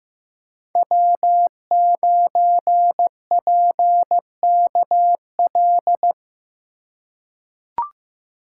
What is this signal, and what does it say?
Morse code, heard as a keyed tone, about 15 wpm, 700 Hz. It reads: W9PKL
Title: USA Call Signs - 5 Characters - 2x Character Spacing - 15wpm